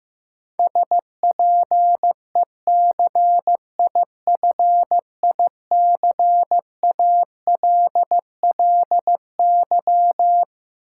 Morse code, heard as a keyed tone, about 15 wpm, 700 Hz. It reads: SPECIFICALLY
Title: Sentences from Top 1100 Words - New Word Drill Alt - 15wpm